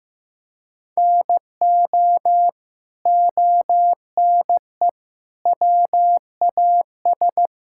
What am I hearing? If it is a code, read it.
Morse code, heard as a keyed tone, about 16 wpm, 700 Hz. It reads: NO ONE WAS
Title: Sentences from Top 1000 Words - Review Part 5 - 15wpm